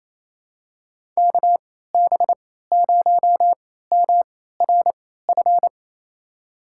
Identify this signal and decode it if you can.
Morse code, heard as a keyed tone, about 28 wpm, spaced as for 16 wpm, 700 Hz. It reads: KB0MRF
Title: USA Call Signs - 6 Characters - 3x Character Spacing - 28wpm